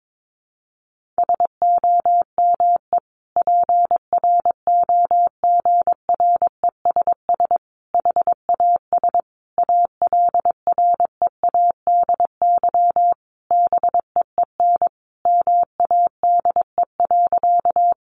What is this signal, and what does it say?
Morse code, heard as a keyed tone, about 22 wpm, 700 Hz. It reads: SOME PROGRESS HAS ALREADY BEEN MADE.
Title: Sentences from Top 1400 Words 22wpm